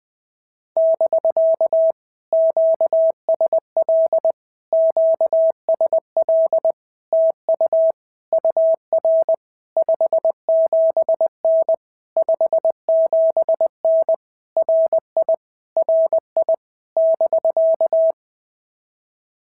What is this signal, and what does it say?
Morse code, heard as a keyed tone, about 20 wpm, 650 Hz. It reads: <BK> QSL QSL TU UR 57N 57N RI RI <BK>
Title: POTA QSO Practice - 20wpm